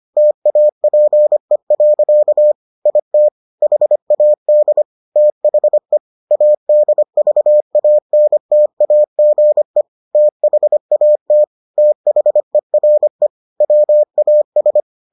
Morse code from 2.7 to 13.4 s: IT HAD THE ADVANTAGE THAT THERE

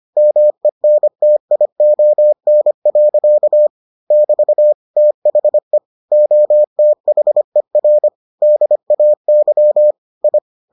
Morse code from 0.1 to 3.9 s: MENTION.

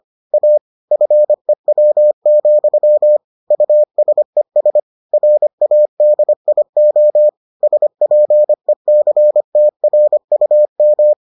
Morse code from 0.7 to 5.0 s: FEW, USES

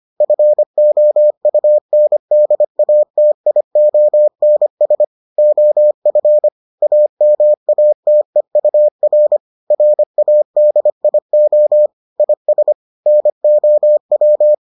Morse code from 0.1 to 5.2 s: FOUNDATIONS

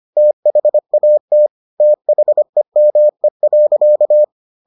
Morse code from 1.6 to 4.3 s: THEME.

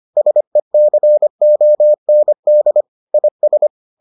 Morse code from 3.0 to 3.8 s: IS

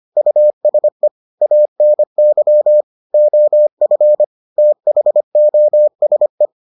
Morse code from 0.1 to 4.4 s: USE ANY OF

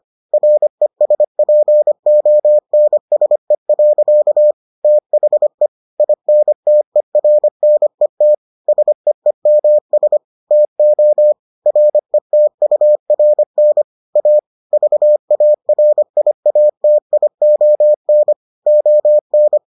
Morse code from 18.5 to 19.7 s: ON

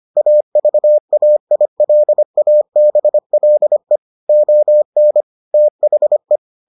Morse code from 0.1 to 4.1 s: AVAILABLE

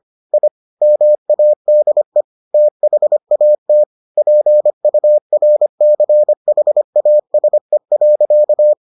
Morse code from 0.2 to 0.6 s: I

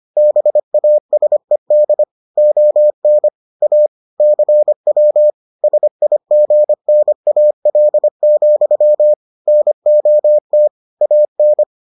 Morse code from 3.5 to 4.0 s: A

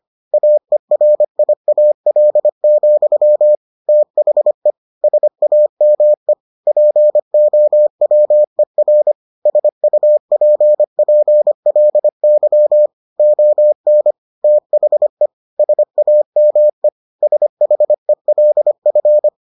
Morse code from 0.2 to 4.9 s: AERIAL, THE

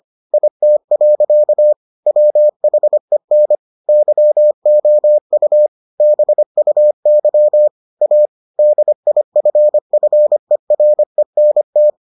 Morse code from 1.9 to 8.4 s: WHEN YOU BUY A

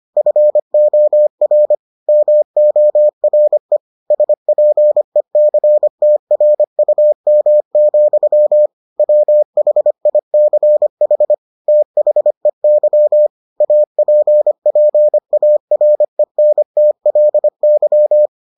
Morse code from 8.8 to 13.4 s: WHICH THEY

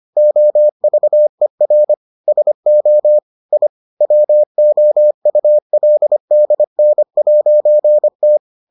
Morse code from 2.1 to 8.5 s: SO I WOULDN'T

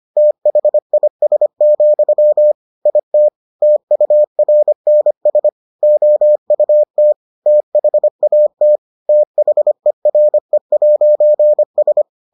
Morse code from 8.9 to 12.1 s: THERE'S

